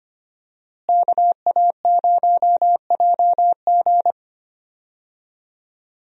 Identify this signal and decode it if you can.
Morse code, heard as a keyed tone, about 25 wpm, 700 Hz. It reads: KA0JG
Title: USA Call Signs - 5 Characters - Repeated 3x - 25wpm